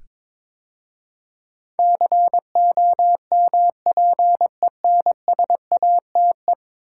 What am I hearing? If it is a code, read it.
Morse code, heard as a keyed tone, about 21 wpm, 700 Hz. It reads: COMPENSATE
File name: practice / 22wpm / 10 Letter Words - First 3 Letters Given - 22wpm